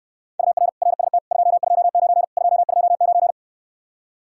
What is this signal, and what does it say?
Morse code, heard as a keyed tone, about 69 wpm, 700 Hz. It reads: UR RST 229 229